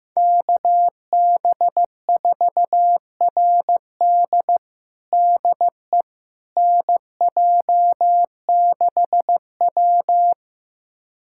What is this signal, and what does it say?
Morse code, heard as a keyed tone, about 15 wpm, 700 Hz. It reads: KB4RD DE NJ6W
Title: USA Callsign DE Callsign 15wpm